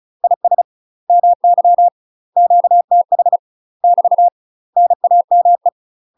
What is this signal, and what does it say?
Morse code, heard as a keyed tone, about 35 wpm, 700 Hz. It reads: IS MY QTH <BT> NAME
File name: UK MASTER QSOs TRAINING / 35WPM / UK MASTER QSOs 0036